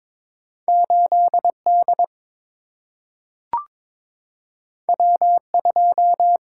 Morse code, heard as a keyed tone, about 22 wpm, 700 Hz. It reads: 8D W2
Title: Single-Letter Single-Number 22wpm